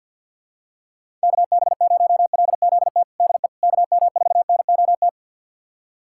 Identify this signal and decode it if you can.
Morse code, heard as a keyed tone, about 51 wpm, 700 Hz. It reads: KB0LZT DE KM4NOT
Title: USA Callsign DE Callsign - 6 Characters - 50wpm